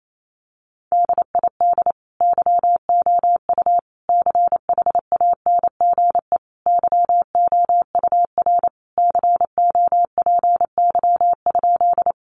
Morse code, heard as a keyed tone, about 28 wpm, 700 Hz. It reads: DID YOU CHANGE YOUR COPY?
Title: Sentences from Top 1400 Words 28wpm